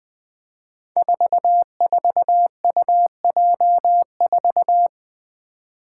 Morse code, heard as a keyed tone, about 20 wpm, 700 Hz. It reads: 44UJ4
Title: Five Challenging Letter-Number Combinations 20wpm